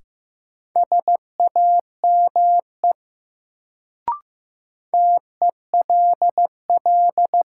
Morse code, heard as a keyed tone, about 15 wpm, 700 Hz. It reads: SAME TELL
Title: Sets of 2 Words - New Word Drill - 15wpm